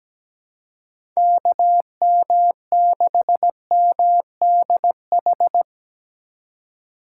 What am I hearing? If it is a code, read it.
Morse code, heard as a keyed tone, about 17 wpm, 700 Hz. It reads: KM6MDH
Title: USA Call Signs - 6 Characters - 6x Character Spacing - 17wpm